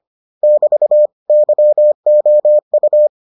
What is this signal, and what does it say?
Morse code, heard as a keyed tone, about 25 wpm, 600 Hz. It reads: <BT> YOU